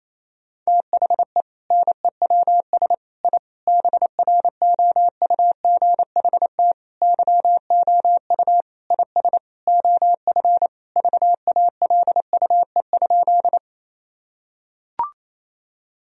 Morse code, heard as a keyed tone, about 27 wpm, 690 Hz. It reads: THE NEWS I BROUGHT YOU IS OF VALUE?
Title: Sentences from Top 1000 Words - Review Part 3 - 28wpm